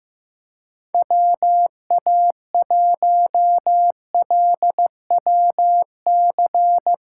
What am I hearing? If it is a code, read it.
Morse code, heard as a keyed tone, about 15 wpm, 700 Hz. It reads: WA1LWC
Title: K1USN SST QSO Practice - 15wpm